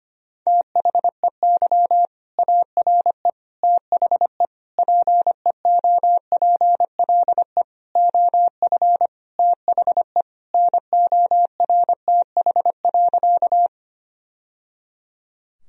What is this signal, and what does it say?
Morse code, heard as a keyed tone, about 25 wpm, 700 Hz. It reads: THEY ARE THE PEOPLE OF THE NORTH.
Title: Sentences from Top 1000 Words - Review Part 1 - 25wpm